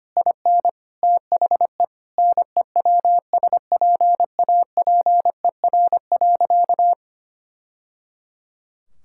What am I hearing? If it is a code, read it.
Morse code, heard as a keyed tone, about 25 wpm, 700 Hz. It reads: IN THE NEWSPAPER.
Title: Sentences from Top 1000 Words - Review Part 5 - 25wpm